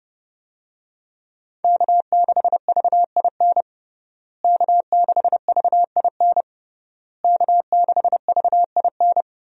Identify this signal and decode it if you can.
Morse code, heard as a keyed tone, about 32 wpm, 700 Hz. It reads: K6VIN K6VIN K6VIN
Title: USA Call Signs - 5 Characters - Repeated 3x - 30wpm